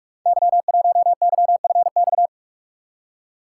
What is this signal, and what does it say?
Morse code, heard as a keyed tone, about 44 wpm, 700 Hz. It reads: Y1YFX
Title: Code Groups - 5 Letter:Numbers - 45wpm